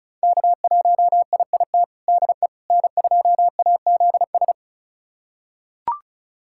Morse code, heard as a keyed tone, about 36 wpm, 700 Hz. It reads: K1IIT DE N2AZS
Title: USA Callsign DE Callsign - 5 Characters - 35wpm